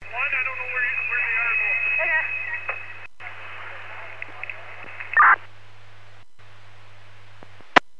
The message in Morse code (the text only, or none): none